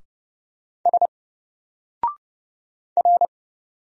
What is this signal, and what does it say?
Morse code, heard as a keyed tone, about 30 wpm, 700 Hz. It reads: S R
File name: Single-Letter-Number-30wpm